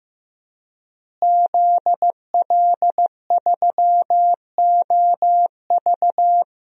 Morse code, heard as a keyed tone, about 16 wpm, 700 Hz. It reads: ZL3OV
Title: DXCC-100 Call Signs - 5 Characters - Repeated 3x - 15wpm